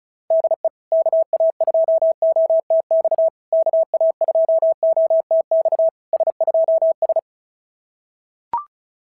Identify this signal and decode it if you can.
Morse code, heard as a keyed tone, about 35 wpm, 660 Hz. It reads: DE KA2OTX KA2OTX S2S
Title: SOTA QSO Practice 35wpm